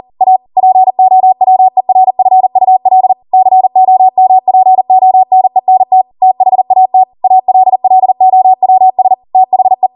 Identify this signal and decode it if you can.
Morse code, heard as a keyed tone, about 40 wpm, 750 Hz. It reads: A POWERFUL COMPONENT THAT ALLOWS THE